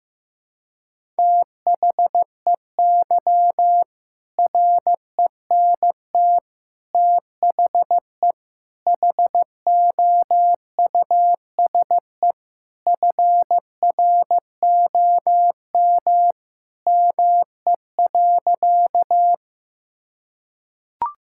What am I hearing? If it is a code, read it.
Morse code, heard as a keyed tone, about 15 wpm, 700 Hz. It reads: THEY RENT THE HOUSE FROM ME.
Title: Sentences from Top 1000 Words - Review Part 5 - 15wpm